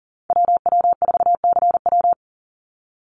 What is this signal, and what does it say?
Morse code, heard as a keyed tone, about 40 wpm, 700 Hz. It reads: WW4CW
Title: USA Call Signs 40wpm